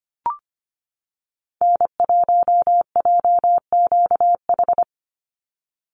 Morse code, heard as a keyed tone, about 25 wpm, 700 Hz. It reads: N1JQH
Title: USA Call Signs 25wpm